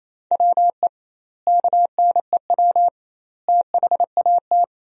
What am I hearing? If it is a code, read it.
Morse code, heard as a keyed tone, about 27 wpm, 700 Hz. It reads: WE KNEW THAT
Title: 2x Word-Spacing - Sentences from Top 400 Words - 28wpm